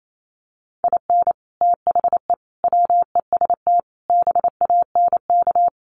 Morse code, heard as a keyed tone, about 28 wpm, 700 Hz. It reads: IN THE WEST BANK
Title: Sets of 5 Words Encore2 28wpm